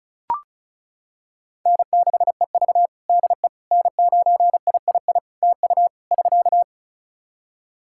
Morse code, heard as a keyed tone, about 36 wpm, 700 Hz. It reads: N6EV DE N9III TU <SK>
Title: Practice QSO - Exchange 4 - Ending - 35wpm